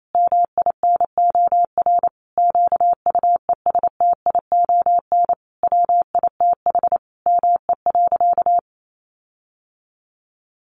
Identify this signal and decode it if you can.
Morse code, heard as a keyed tone, about 28 wpm, 700 Hz. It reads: MINOR QUESTION WITH ME.
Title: Sentences from Top 1500 Words 28wpm